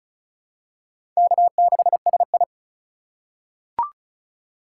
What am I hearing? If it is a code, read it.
Morse code, heard as a keyed tone, about 36 wpm, 700 Hz. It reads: K6SI
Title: USA Call Signs - 4 Characters - 4x Character Spacing - 35wpm